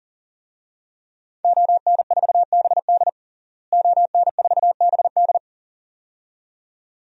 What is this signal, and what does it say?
Morse code, heard as a keyed tone, about 40 wpm, 700 Hz. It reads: ON4BD ON4BD
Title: DXCC-100 Call Signs - 5 Characters - Repeated 2x - 40wpm